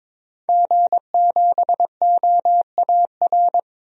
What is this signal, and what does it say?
Morse code, heard as a keyed tone, about 22 wpm, 700 Hz. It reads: G7OAR
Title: DXCC-100 Call Signs - 5 Characters - Repeated 3x - 22wpm